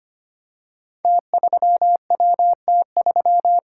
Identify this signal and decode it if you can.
Morse code, heard as a keyed tone, about 25 wpm, 700 Hz. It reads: T3WT3
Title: Code Groups - 5 Letter:Numbers - 25wpm